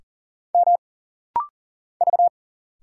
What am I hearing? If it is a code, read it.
Morse code, heard as a keyed tone, about 40 wpm, 700 Hz. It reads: M V
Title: Single Letters - Rapid-Fire 40wpm